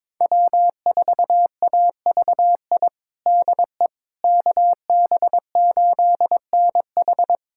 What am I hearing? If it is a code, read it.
Morse code, heard as a keyed tone, about 22 wpm, 700 Hz. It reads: W4AVI DE KB8NH
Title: USA Callsign DE Callsign - 5 Characters - 22wpm